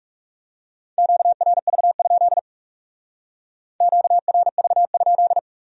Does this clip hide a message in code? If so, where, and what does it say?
Yes: Morse code from 1.0 to 5.4 s, heard as a keyed tone, about 38 wpm, 700 Hz. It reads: QRV? QRV?